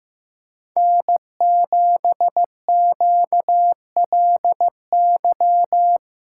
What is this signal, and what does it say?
Morse code, heard as a keyed tone, about 17 wpm, 700 Hz. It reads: N7QLY
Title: Speed Racing - USA 1x3 Call Signs 15wpm